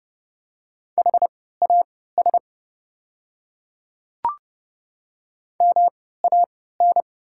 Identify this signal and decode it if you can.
Morse code, heard as a keyed tone, about 30 wpm, spaced as for 17 wpm, 690 Hz. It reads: HAS MAN
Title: 3x Character Spacing - Top 200 Words - 30wpm